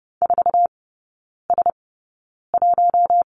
Morse code, heard as a keyed tone, about 30 wpm, spaced as for 9 wpm, 700 Hz. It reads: 4S1
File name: 081 Lesson 14e - Number 2 - Three Characters 30wpm